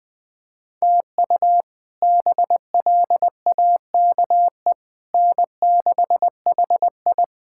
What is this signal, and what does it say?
Morse code, heard as a keyed tone, about 20 wpm, 700 Hz. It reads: TU BLAKE N6HI
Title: K1USN SST QSO Practice - 20wpm